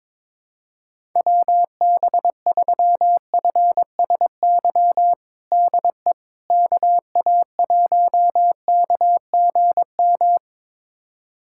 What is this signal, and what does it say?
Morse code, heard as a keyed tone, about 22 wpm, 700 Hz. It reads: WB3FSY DE KA1KGM
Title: USA Callsign DE Callsign - 6 Characters - 22wpm